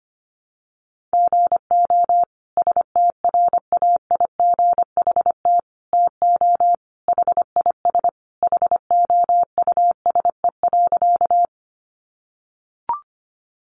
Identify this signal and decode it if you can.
Morse code, heard as a keyed tone, about 25 wpm, 700 Hz. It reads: GO STRAIGHT TO HIS HOUSE.
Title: Sentences from Top 1000 Words 25wpm